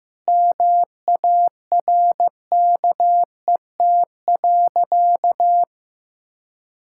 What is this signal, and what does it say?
Morse code, heard as a keyed tone, about 16 wpm, 700 Hz. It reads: MARKET.
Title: Sentences from Top 1000 Words - Review Part 6 - 15wpm